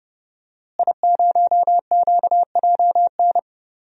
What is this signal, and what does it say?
Morse code, heard as a keyed tone, about 30 wpm, 700 Hz. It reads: I0QJN
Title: DXCC-100 Call Signs - 5 Characters - Repeated 3x - 30wpm